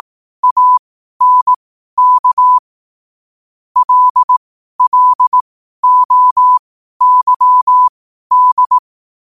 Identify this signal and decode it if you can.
Morse code, heard as a keyed tone, about 18 wpm, spaced as for 12 wpm, 1000 Hz. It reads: ANK LLOYD
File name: qotdfiles / 12wpm